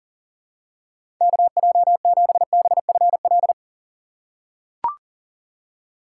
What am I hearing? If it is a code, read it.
Morse code, heard as a keyed tone, about 40 wpm, 700 Hz. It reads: KJ7BFL
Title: USA Call Signs - 6 Characters - 2x Character Spacing - 40wpm